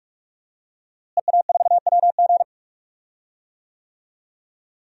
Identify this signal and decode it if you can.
Morse code, heard as a keyed tone, about 46 wpm, 700 Hz. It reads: EA4WG
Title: DXCC-100 Call Signs - 5 Characters - Repeated 3x - 45wpm